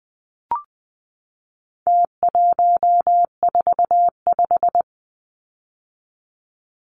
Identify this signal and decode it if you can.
Morse code, heard as a keyed tone, about 20 wpm, 700 Hz. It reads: T145
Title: Numbers - Contest Style - 4-Digit 20wpm